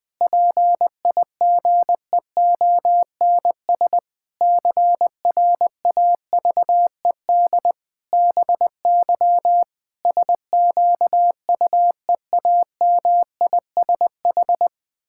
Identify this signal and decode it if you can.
Morse code, heard as a keyed tone, about 20 wpm, 700 Hz. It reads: PIGEONS CRAVED BY SQUEAMISH
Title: Pangram Challenge v2 - 20wpm